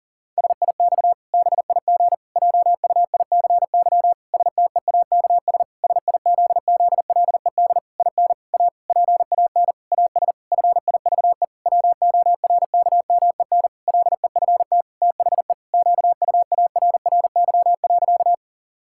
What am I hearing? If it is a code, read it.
Morse code, heard as a keyed tone, about 37 wpm, 700 Hz. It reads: SIX BIG JUICY STEAKS SIZZLED IN A PAN AS FIVE WORKMEN LEFT THE QUARRY.
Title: Pangram Challenge v2 - 40wpm